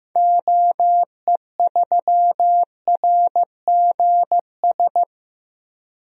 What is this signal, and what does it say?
Morse code, heard as a keyed tone, about 15 wpm, 690 Hz. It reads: OE3RGS
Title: DXCC-100 Call Signs - 6 Characters - 15wpm